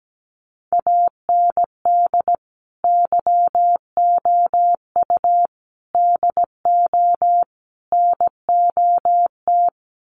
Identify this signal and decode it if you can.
Morse code, heard as a keyed tone, about 17 wpm, 700 Hz. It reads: AND YOU DO NOT
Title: Sets of 4 Words 17wpm